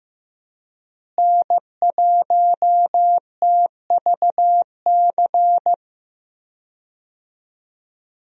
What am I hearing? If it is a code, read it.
Morse code, heard as a keyed tone, about 15 wpm, 700 Hz. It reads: N1TVC
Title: USA Call Signs - 5 Characters - Repeated 2x - 15wpm